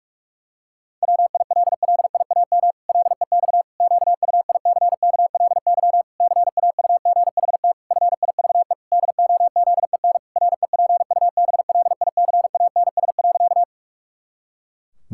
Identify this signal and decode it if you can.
Morse code, heard as a keyed tone, about 48 wpm, 700 Hz. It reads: WILLIAM LEX QUICKLY CAUGHT FIVE DOZEN REPUBLICANS.
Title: Pangram Challenge v2 - 45wpm